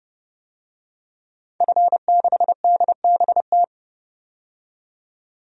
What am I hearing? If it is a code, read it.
Morse code, heard as a keyed tone, about 32 wpm, 700 Hz. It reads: F6DBT
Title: DXCC-100 Call Signs - 5 Characters - 30wpm